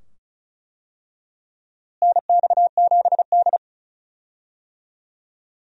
Morse code, heard as a keyed tone, about 36 wpm, 700 Hz. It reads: NX7D
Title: USA Call Signs - 4 Characters - Repeated 2x - 35wpm